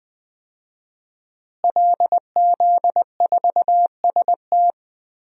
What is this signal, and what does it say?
Morse code, heard as a keyed tone, about 20 wpm, 700 Hz. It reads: LZ4ST
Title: DXCC-100 Call Signs - 5 Characters - Repeated 2x - 20wpm